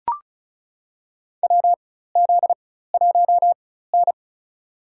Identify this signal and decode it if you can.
Morse code, heard as a keyed tone, about 36 wpm, spaced as for 16 wpm, 700 Hz. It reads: WZ1N
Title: USA Call Signs - 4 Characters - 4x Character Spacing - 35wpm